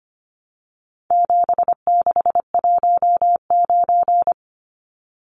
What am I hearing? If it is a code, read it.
Morse code, heard as a keyed tone, about 25 wpm, 700 Hz. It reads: 7619
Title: Numbers - 4-Digit 25wpm